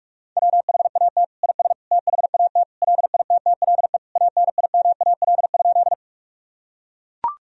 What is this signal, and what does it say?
Morse code, heard as a keyed tone, about 41 wpm, 700 Hz. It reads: WHAT IS THAT LITTLE ANIMAL?
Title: Sentences from Top 1000 Words - Review Part 1 - 45wpm